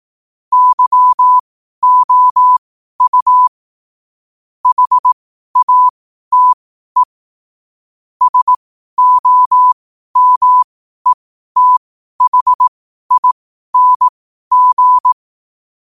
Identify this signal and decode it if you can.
Morse code, heard as a keyed tone, about 18 wpm, spaced as for 12 wpm, 1000 Hz. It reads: YOU HATE SOMETHING
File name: qotdfiles / 12wpm